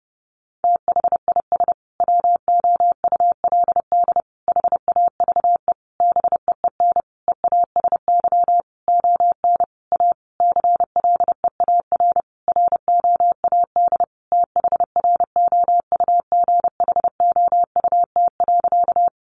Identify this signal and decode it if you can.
Morse code, heard as a keyed tone, about 30 wpm, 700 Hz. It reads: THIS WOULD HAVE BEEN EASY ON A CLEAR ROAD THROUGHOUT.
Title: Sentences from Top 800 Words 30wpm